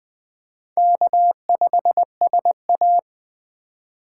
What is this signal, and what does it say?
Morse code, heard as a keyed tone, about 21 wpm, 700 Hz. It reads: K5SA
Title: Speed Racing - USA 1x2 Call Signs 17wpm